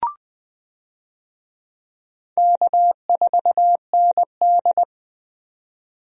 Morse code, heard as a keyed tone, about 20 wpm, 700 Hz. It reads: K4ND
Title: USA Call Signs - 4 Characters - 20wpm